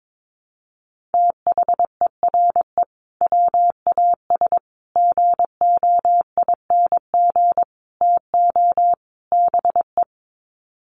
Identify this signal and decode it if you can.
Morse code, heard as a keyed tone, about 22 wpm, 700 Hz. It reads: THERE WAS GOING TO BE
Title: Sets of 5 Words Encore2 22wpm